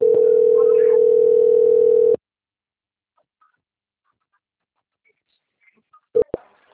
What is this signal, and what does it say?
Ringback tone (North America), 440 + 480 Hz.